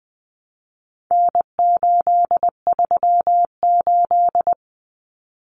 Morse code, heard as a keyed tone, about 19 wpm, 700 Hz. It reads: N838
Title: Common Cut Numbers - 4-Digit 20wpm